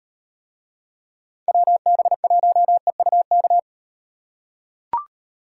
Morse code, heard as a keyed tone, about 35 wpm, 700 Hz. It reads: WB1EUK